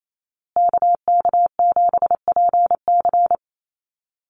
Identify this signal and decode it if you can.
Morse code, heard as a keyed tone, about 28 wpm, 700 Hz. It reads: KK7PC
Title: USA Call Signs Repeated 3x 28wpm